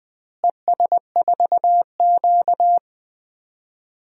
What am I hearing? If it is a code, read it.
Morse code, heard as a keyed tone, about 21 wpm, 700 Hz. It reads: ES4Q
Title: DXCC-100 Call Signs - 4 Characters - Repeated 2x - 20wpm